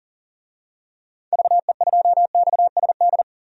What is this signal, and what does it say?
Morse code, heard as a keyed tone, about 38 wpm, 700 Hz. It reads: VE2XSD
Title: DXCC-100 Call Signs - 6 Characters - 40wpm